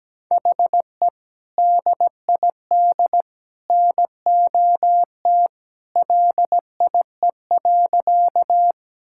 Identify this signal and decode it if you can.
Morse code, heard as a keyed tone, about 17 wpm, 700 Hz. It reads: HE DID NOT LIE.
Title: Sentences from Top 1600 Words 17wpm